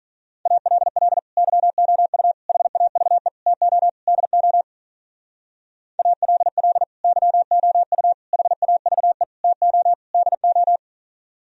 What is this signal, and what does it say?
Morse code, heard as a keyed tone, about 45 wpm, 700 Hz. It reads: ALL YOU HAVE TO DO ALL YOU HAVE TO DO